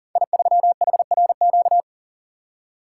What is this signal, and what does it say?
Morse code, heard as a keyed tone, about 42 wpm, 700 Hz. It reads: I3HRQ
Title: Code Groups - 5 Letter:Numbers - 40wpm